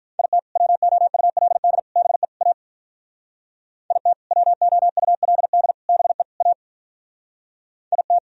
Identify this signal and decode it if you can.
Morse code, heard as a keyed tone, about 47 wpm, 700 Hz. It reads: IT WOULD BE A IT WOULD BE A IT